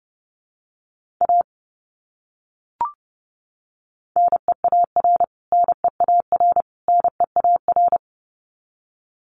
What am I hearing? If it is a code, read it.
Morse code, heard as a keyed tone, about 30 wpm, 700 Hz. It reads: A NEAR NEAR NEAR
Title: Top 200 Words Repeated 30wpm